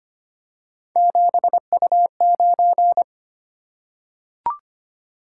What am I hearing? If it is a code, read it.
Morse code, heard as a keyed tone, about 25 wpm, 700 Hz. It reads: 7U9
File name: Three Letter-Number Combinations - 25wpm